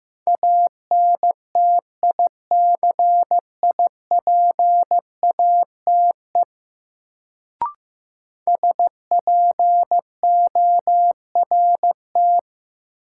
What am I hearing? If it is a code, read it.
Morse code, heard as a keyed tone, about 15 wpm, 700 Hz. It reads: ANTICIPATE SPORT
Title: Sentences from Top 1500 Words - New Word Drill Alt - 15wpm